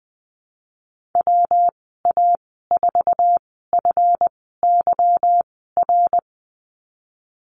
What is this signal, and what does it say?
Morse code, heard as a keyed tone, about 20 wpm, spaced as for 14 wpm, 700 Hz. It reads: WA4FYR
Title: 186 Lesson 30g - Letter b - Call Signs 20wpm